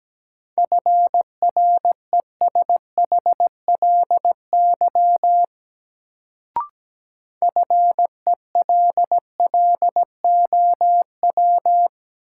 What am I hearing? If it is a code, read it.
Morse code, heard as a keyed tone, about 17 wpm, 700 Hz. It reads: FRESHLY FELLOW
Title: Sets of 5 Words Encore - New Word Drill - 17wpm